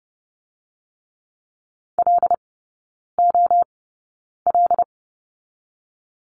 Morse code, heard as a keyed tone, about 30 wpm, spaced as for 9 wpm, 700 Hz. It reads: LOL
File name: 081 Lesson 14e - Number 2 - Three Characters 30wpm